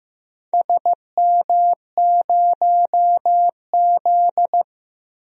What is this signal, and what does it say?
Morse code, heard as a keyed tone, about 16 wpm, 700 Hz. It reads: SM0Z
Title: DXCC-100 Call Signs - 4 Characters - Repeated 3x - 15wpm